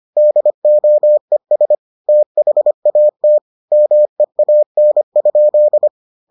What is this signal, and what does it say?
Morse code, heard as a keyed tone, about 25 wpm, 600 Hz. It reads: DOES THAT MEAN?